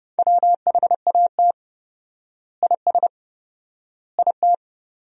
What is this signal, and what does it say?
Morse code, heard as a keyed tone, about 30 wpm, 700 Hz. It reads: WHAT IS IT
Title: 4x Word-Spacing - Sentences from Top 100 Words - 30wpm